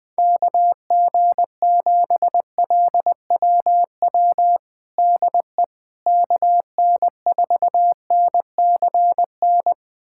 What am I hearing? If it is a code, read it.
Morse code, heard as a keyed tone, about 20 wpm, 700 Hz. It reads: KG7LWW DE KN4NCN
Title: USA Callsign DE Callsign 20wpm